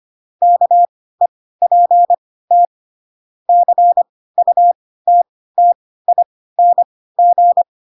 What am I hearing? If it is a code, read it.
Morse code, heard as a keyed tone, about 25 wpm, spaced as for 16 wpm, 700 Hz. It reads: KEPT CUTTING
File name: Pinocchio / Pinocchio ch3 10 25 ik1hge